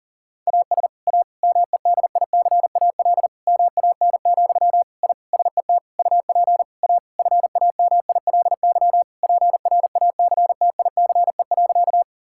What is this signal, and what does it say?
Morse code, heard as a keyed tone, about 40 wpm, 700 Hz. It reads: AS A MEDICAL MAN, I SET UP A FAMILY PRACTICE.
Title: Sentences from Top 1000 Words - Review Part 5 - 40wpm